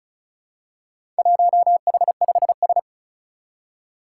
Morse code, heard as a keyed tone, about 36 wpm, 700 Hz. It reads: 1H5S
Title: Four Challenging Letter-Number Combinations 35wpm